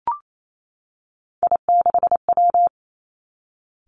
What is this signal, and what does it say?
Morse code, heard as a keyed tone, about 28 wpm, 700 Hz. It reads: I6W
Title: Three Challenging Letter-Number Combinations 28wpm